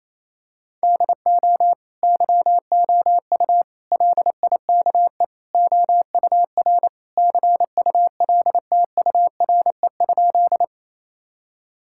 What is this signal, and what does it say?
Morse code, heard as a keyed tone, about 28 wpm, 700 Hz. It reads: DO YOU LIKE OUR CULTURE?
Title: Sentences from Top 1000 Words - Review Part 6 - 28wpm